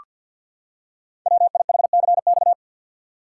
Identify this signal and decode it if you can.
Morse code, heard as a keyed tone, about 48 wpm, 690 Hz. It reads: WIHCX
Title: Code Groups - 5 Letters - 50wpm